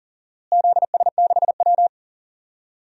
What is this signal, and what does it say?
Morse code, heard as a keyed tone, about 39 wpm, 700 Hz. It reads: ZS6W
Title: DXCC-100 Call Signs - 4 Characters - Repeated 3x - 40wpm